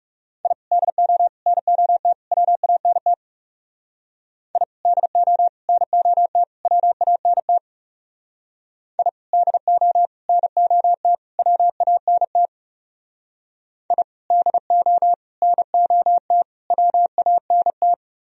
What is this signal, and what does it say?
Morse code, heard as a keyed tone, about 37 wpm, 700 Hz. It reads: I DO NOT WANT I DO NOT WANT I DO NOT WANT I DO NOT WANT
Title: Speed-Racing - Sets of 4 Words 30wpm